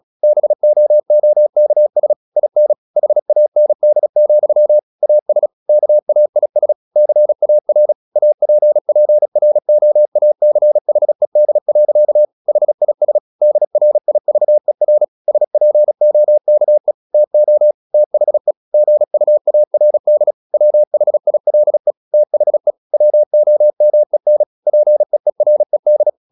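Morse code, heard as a keyed tone, about 36 wpm, 600 Hz. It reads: BOOKS IN HAND, AS KAIS CAR APPROACHED. HIS DRIVER SPOKE TO THE GUARD WHILE THE WOMEN PEERED
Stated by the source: Morse practice at 36 wpm, text of Ken Follett's novel Never, from CW med Gnister.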